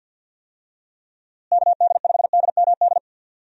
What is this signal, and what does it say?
Morse code, heard as a keyed tone, about 51 wpm, 700 Hz. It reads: KD5DMD
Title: USA Call Signs - 6 Characters - Repeated 2x - 50wpm